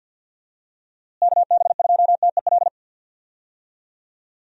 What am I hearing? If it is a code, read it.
Morse code, heard as a keyed tone, about 48 wpm, 700 Hz. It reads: KB2TEL
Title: USA Call Signs - 6 Characters - Repeated 3x - 50wpm